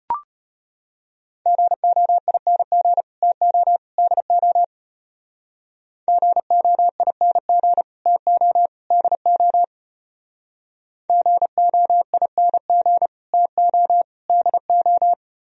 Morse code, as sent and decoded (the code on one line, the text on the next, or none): --. --- .. -. --. / - --- / -.. --- / --. --- .. -. --. / - --- / -.. --- / --. --- .. -. --. / - --- / -.. ---
GOING TO DO GOING TO DO GOING TO DO